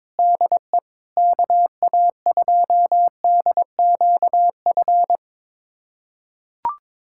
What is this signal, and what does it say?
Morse code, heard as a keyed tone, about 22 wpm, 700 Hz. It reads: DE KA2DQF
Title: USA Callsign DE Callsign - 6 Characters - 22wpm